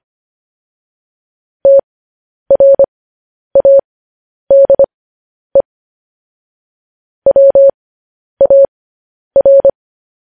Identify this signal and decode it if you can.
Morse code, heard as a keyed tone, about 25 wpm, spaced as for 10 wpm, 550 Hz. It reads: TRADE WAR
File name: podcast / News headlines-1x-25x10